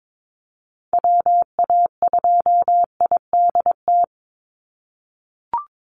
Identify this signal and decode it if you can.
Morse code, heard as a keyed tone, about 22 wpm, 700 Hz. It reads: WA2IDT
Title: USA Call Signs 22wpm